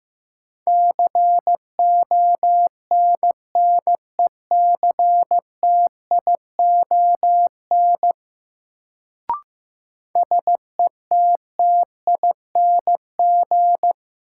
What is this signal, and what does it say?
Morse code, heard as a keyed tone, about 15 wpm, 700 Hz. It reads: CONNECTION SETTING
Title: Sentences from Top 1300 Words - New Word Drill - 15wpm